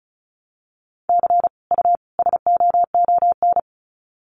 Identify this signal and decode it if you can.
Morse code, heard as a keyed tone, about 34 wpm, 700 Hz. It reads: C U SOON
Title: Common QSO Phrases 35wpm